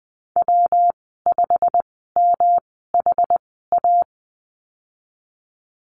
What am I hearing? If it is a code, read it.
Morse code, heard as a keyed tone, about 20 wpm, spaced as for 15 wpm, 700 Hz. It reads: W5MHA
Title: 122 Lesson 20g - Number 3 - Call Signs 20wpm